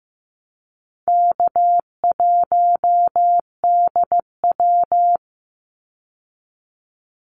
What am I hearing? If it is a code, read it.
Morse code, heard as a keyed tone, about 15 wpm, 700 Hz. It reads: K1DW
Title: Easy Call Signs - 15wpm